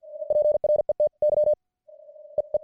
Morse code, heard as a keyed tone, about 50 wpm, spaced as for 43 wpm, 600 Hz. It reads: PRETY EE